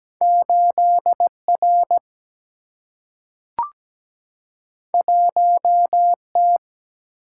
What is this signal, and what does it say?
Morse code, heard as a keyed tone, about 17 wpm, 700 Hz. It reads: 8R 1T
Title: Single-Letter Single-Number 17wpm